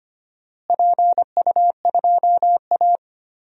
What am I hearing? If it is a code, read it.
Morse code, heard as a keyed tone, about 25 wpm, 700 Hz. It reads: PU2A
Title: DXCC-100 Call Signs - 4 Characters - Repeated 2x - 25wpm